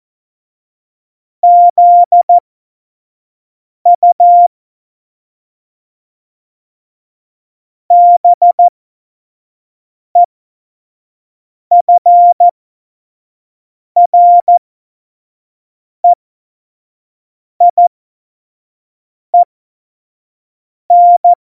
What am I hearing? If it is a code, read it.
Morse code, heard as a keyed tone, about 14 wpm, spaced as for 5 wpm, 700 Hz. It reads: ZU BEFREIEN
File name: Morsekurs / Witze / Witz 15. Mai 2018, 25 F70